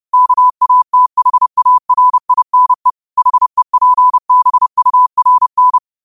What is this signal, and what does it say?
Morse code, heard as a keyed tone, about 30 wpm, 1000 Hz. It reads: KATHARINE HEPBURN